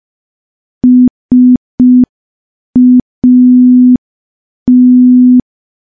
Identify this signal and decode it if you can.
Morse code, heard as a keyed tone, about 5 wpm, 250 Hz. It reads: SAT
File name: morse / morse-training-05WPM